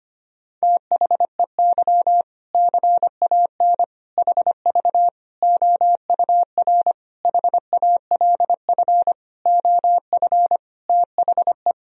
Morse code, heard as a keyed tone, about 25 wpm, 700 Hz. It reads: THEY CAN HV OUR HALF OF THE
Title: Sentences from Top 400 Words and Abbreviations 25wpm